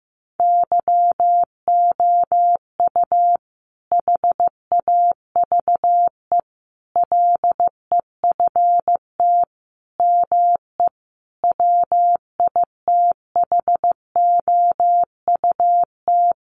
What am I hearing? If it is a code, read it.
Morse code, heard as a keyed tone, about 15 wpm, 700 Hz. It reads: YOU HAVE LEFT ME WITHOUT
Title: Sentences from Top 600 Words 15wpm